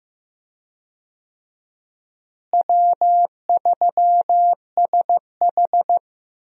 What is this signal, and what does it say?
Morse code, heard as a keyed tone, about 16 wpm, 700 Hz. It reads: W3SH
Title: USA Call Signs - 4 Characters - Repeated 3x - 15wpm